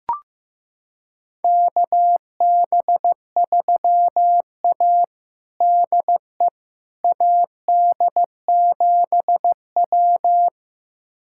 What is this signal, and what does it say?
Morse code, heard as a keyed tone, about 14 wpm, 700 Hz. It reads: KB3A DE AD7W
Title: USA Callsign DE Callsign - 4 Characters - 15wpm